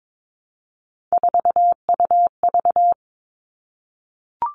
Morse code, heard as a keyed tone, about 22 wpm, 700 Hz. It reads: 4UV
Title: All Cut Numbers - 3-Digit 22wpm